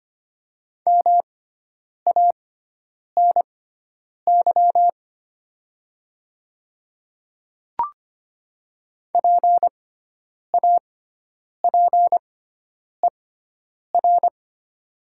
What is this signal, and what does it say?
Morse code, heard as a keyed tone, about 25 wpm, spaced as for 9 wpm, 690 Hz. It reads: MANY PAPER
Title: 6x Character Spacing - Top 300 Words - 25wpm